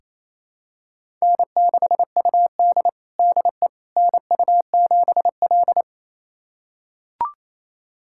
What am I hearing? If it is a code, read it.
Morse code, heard as a keyed tone, about 27 wpm, 700 Hz. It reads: N6UD DE NU7L
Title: USA Callsign DE Callsign - 4 Characters - 28wpm